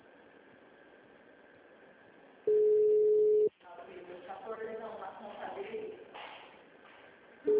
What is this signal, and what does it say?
Ringback tone (Europe, 425 Hz), with speech.
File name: out- -2006- - - .4342